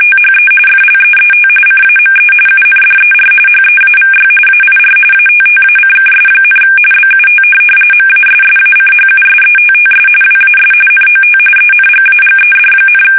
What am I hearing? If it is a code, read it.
RTTY, about 45.45 baud. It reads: THE FLEET CAME UNDER HEAVY AIR ATTACKS, BUT MASSACHUSETTS WAS NOT ENGAGED,
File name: fsk850 / USS-Massachusetts